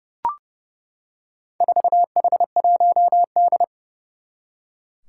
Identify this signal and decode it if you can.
Morse code, heard as a keyed tone, about 30 wpm, 690 Hz. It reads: 4H1D
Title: Four Challenging Letter-Number Combinations 30wpm